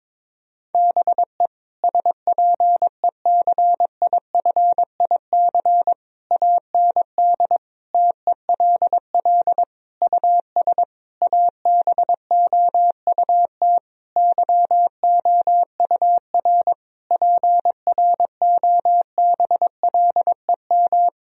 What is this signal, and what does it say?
Morse code, heard as a keyed tone, about 22 wpm, 700 Hz. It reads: BE SPECIFIC AND TELL US ABOUT YOUR PROBLEM
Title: Sentences from Top 1000 Words - Review Part 3 - 22wpm